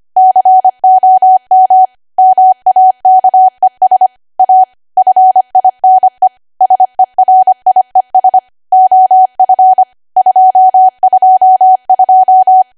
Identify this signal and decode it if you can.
Morse code, heard as a keyed tone, about 25 wpm, 750 Hz. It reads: COM MAKES A FINE SERIES OF 222
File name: Archive / 25 WPM / 25WPM